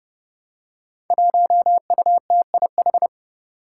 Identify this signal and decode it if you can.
Morse code, heard as a keyed tone, about 32 wpm, 700 Hz. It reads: 1UTIH
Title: Five Challenging Letter-Number Combinations 30wpm